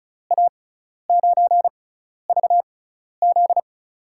Morse code, heard as a keyed tone, about 36 wpm, spaced as for 12 wpm, 700 Hz. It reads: A9VZ
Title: USA Call Signs - 5 Characters - 6x Character Spacing - 35wpm